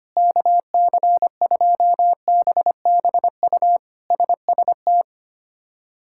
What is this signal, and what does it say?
Morse code, heard as a keyed tone, about 25 wpm, 700 Hz. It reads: KC2BBU SST
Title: K1USN SST QSO Practice - 25wpm